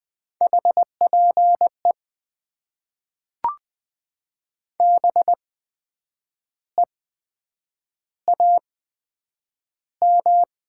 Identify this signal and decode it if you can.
Morse code, heard as a keyed tone, about 20 wpm, spaced as for 8 wpm, 700 Hz. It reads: HPE BEAM
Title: Top 90 QSO Elements - 8x Character Spacing - 20wpm